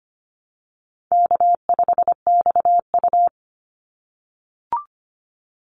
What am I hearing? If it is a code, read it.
Morse code, heard as a keyed tone, about 25 wpm, 700 Hz. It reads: K5XU
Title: Easy Call Signs - 25wpm